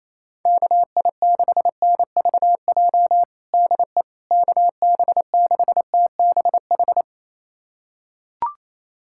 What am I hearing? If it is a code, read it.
Morse code, heard as a keyed tone, about 27 wpm, 700 Hz. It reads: KI6NVJ DE KB6TBH
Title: USA Callsign DE Callsign - 6 Characters - 28wpm